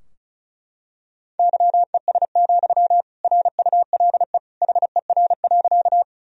Morse code, heard as a keyed tone, about 34 wpm, 700 Hz. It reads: YES, RULE HER.
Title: Sentences from Top 1000 Words - Review Part 2 - 35wpm